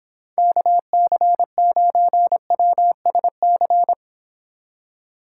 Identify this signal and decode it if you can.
Morse code, heard as a keyed tone, about 30 wpm, 700 Hz. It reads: KC9WSC